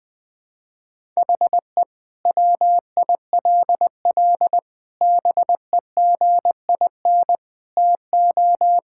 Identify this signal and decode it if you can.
Morse code, heard as a keyed tone, about 20 wpm, 700 Hz. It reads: HE WILL BEGIN TO
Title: Sentences from Top 1600 Words 20wpm